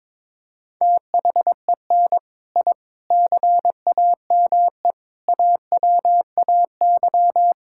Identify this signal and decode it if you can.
Morse code, heard as a keyed tone, about 22 wpm, 700 Hz. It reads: THEN I CAME AWAY
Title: Sentences from Top 1000 Words - Review Part 6 - 22wpm